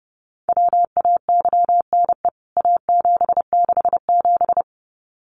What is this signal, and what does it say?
Morse code, heard as a keyed tone, about 30 wpm, 700 Hz. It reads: WAYNE A767
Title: CWT Exchange 30wpm